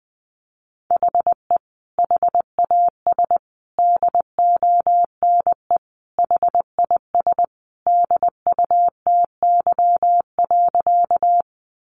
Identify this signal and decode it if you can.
Morse code, heard as a keyed tone, about 20 wpm, 700 Hz. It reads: HE HAS DONE HIS DUTY.
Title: Sentences from Top 1500 Words 20wpm